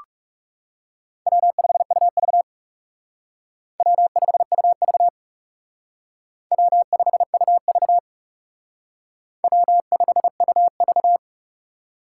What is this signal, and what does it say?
Morse code, heard as a keyed tone, about 37 wpm, 700 Hz. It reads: W5UV W5UV W5UV W5UV
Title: Speed Racing - USA 1x2 Call Signs 30wpm